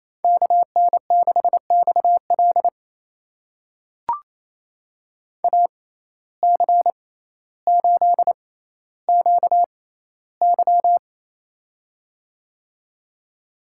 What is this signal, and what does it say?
Morse code, heard as a keyed tone, about 27 wpm, spaced as for 14 wpm, 700 Hz. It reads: KN6XL AC8QY
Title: USA Call Signs - 5 Characters - 6x Character Spacing - 28wpm